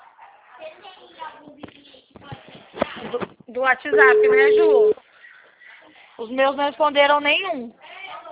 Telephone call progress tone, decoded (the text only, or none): ringback tone